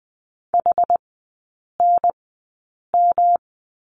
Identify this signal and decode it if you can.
Morse code, heard as a keyed tone, about 20 wpm, spaced as for 8 wpm, 700 Hz. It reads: HNM
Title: 127 Lesson 21e - Number 6 - Three Characters 20wpm